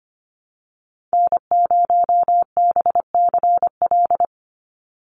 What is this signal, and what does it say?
Morse code, heard as a keyed tone, about 25 wpm, 700 Hz. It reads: N0BCL
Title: USA Call Signs 25wpm